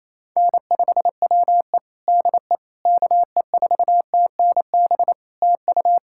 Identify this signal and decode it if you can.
Morse code, heard as a keyed tone, about 28 wpm, 700 Hz. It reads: N5WE DE KE4TNB TU
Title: K1USN SST QSO Practice 28wpm